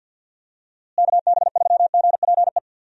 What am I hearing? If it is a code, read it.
Morse code, heard as a keyed tone, about 46 wpm, 700 Hz. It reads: KB3GPE
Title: USA Call Signs - 6 Characters - Repeated 2x - 50wpm